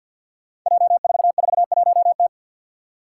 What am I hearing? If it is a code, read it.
Morse code, heard as a keyed tone, about 47 wpm, 700 Hz. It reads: J441T
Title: Five Challenging Letter-Number Combinations 50wpm